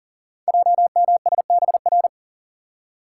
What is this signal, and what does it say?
Morse code, heard as a keyed tone, about 43 wpm, 700 Hz. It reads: JMSBR
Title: Code Groups - 5 Letters - 40wpm